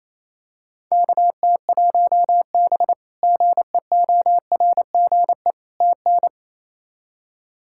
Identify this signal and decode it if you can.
Morse code, heard as a keyed tone, about 27 wpm, 700 Hz. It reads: KT1B GEORGE TN
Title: K1USN SST QSO Practice - 28wpm